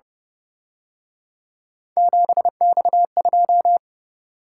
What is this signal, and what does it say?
Morse code, heard as a keyed tone, about 29 wpm, 700 Hz. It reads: 7X2